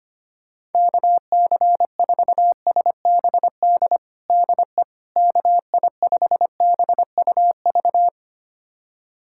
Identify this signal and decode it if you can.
Morse code, heard as a keyed tone, about 25 wpm, 700 Hz. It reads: KC4SBD DE KI5BUV
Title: USA Callsign DE Callsign - 6 Characters - 25wpm